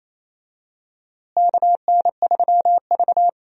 Morse code, heard as a keyed tone, about 28 wpm, 700 Hz. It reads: KN3V
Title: USA Call Signs - 4 Characters - Repeated 3x - 28wpm